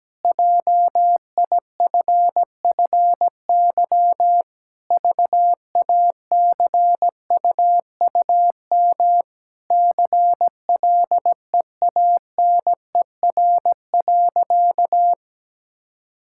Morse code, heard as a keyed tone, about 17 wpm, 700 Hz. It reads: JIFFY VACUUM CLEANER.
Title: Pangram Challenge v2 - 17wpm